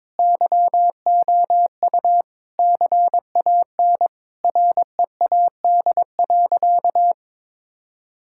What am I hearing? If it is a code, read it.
Morse code, heard as a keyed tone, about 22 wpm, 700 Hz. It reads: YOU CAN READ.
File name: practice / 22wpm / Sentences with 8-Word Phrases - 22wpm